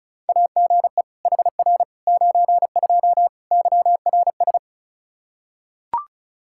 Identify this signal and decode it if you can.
Morse code, heard as a keyed tone, about 36 wpm, 700 Hz. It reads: AGE HR 92 YRS
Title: K1USN SST QSO Practice 35wpm